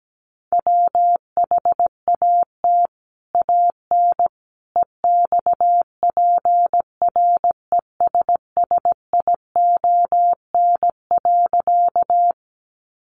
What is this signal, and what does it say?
Morse code, heard as a keyed tone, about 17 wpm, 700 Hz. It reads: WHAT AN EXPRESSION.
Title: Sentences from Top 1200 Words 17wpm